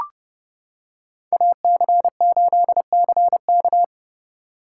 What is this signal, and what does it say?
Morse code, heard as a keyed tone, about 29 wpm, 700 Hz. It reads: AC8CK